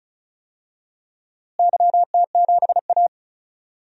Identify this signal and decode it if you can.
Morse code, heard as a keyed tone, about 36 wpm, 700 Hz. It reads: YT7A
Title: DXCC-100 Call Signs - 4 Characters - Repeated 3x - 35wpm